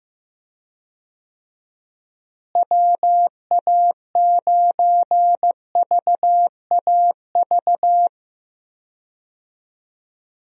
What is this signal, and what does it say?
Morse code, heard as a keyed tone, about 15 wpm, 700 Hz. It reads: WA9VAV
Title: USA Call Signs - 6 Characters - Repeated 3x - 15wpm